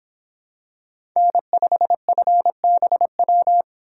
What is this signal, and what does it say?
Morse code, heard as a keyed tone, about 25 wpm, 700 Hz. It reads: N5FBW